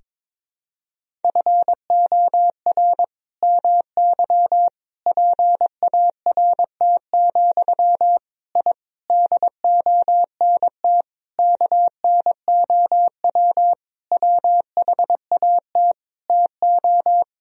Morse code, heard as a keyed tone, about 22 wpm, 700 Hz. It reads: FOR MY PART, I DONT KNOW WHAT TO
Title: Sentences from Top 1000 Words - Review Part 1 - 22wpm